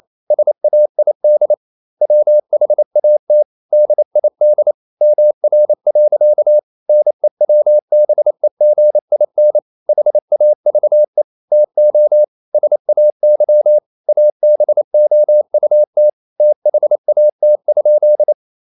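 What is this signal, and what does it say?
Morse code, heard as a keyed tone, about 28 wpm, 600 Hz. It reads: SAID WHAT DID MR. NEWBEGIN HAVE TO SAY ABOUT THAT?
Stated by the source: Morse code practice at 28 wpm, text of Ken Follett's novel Never in English, from CW med Gnister.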